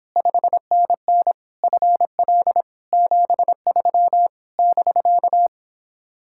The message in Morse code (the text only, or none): 5NN FL 73 <BK>